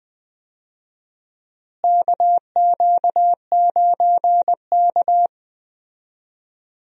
Morse code, heard as a keyed tone, about 20 wpm, 700 Hz. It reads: KQ9K
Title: USA Call Signs - 4 Characters - Repeated 2x - 20wpm